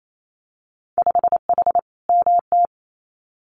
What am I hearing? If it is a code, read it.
Morse code, heard as a keyed tone, about 28 wpm, 700 Hz. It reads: 5H MT
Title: Winter Field Day Exchanges Encore 28wpm